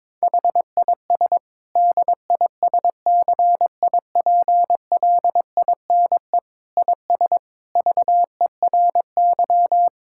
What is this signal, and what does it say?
Morse code, heard as a keyed tone, about 22 wpm, 700 Hz. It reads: HIS DISCIPLINE IS VERY
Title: Sentences from Top 1000 Words - Review Part 6 - 22wpm